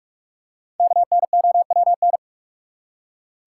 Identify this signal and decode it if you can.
Morse code, heard as a keyed tone, about 42 wpm, 700 Hz. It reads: KNOWN
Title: Five Letter Words - 45wpm